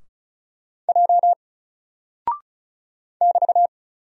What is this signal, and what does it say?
Morse code, heard as a keyed tone, about 34 wpm, 700 Hz. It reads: J <BT>
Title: Single Character (Letter:Number:Punctuation:Prosign) Alt 35wpm